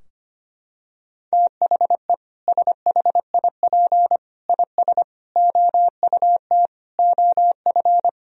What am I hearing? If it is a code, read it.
Morse code, heard as a keyed tone, about 25 wpm, 700 Hz. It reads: THE SHIP IS OUT OF
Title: Sentences from Top 1000 Words - Review Part 4 - 25wpm